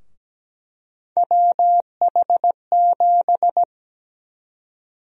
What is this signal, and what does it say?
Morse code, heard as a keyed tone, about 17 wpm, 700 Hz. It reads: WH7
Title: USA Call Sign Prefixes - Repeated 2x - 17wpm